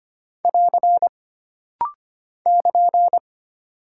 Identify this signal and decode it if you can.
Morse code, heard as a keyed tone, about 25 wpm, 700 Hz. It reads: <AR> <KN>
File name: Single Punctuation-Prosign - Rapid-Fire - 25wpm